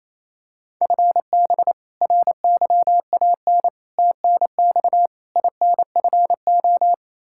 Morse code, heard as a keyed tone, about 28 wpm, 700 Hz. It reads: FB RYAN TNX INFO
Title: K1USN SST QSO Practice 28wpm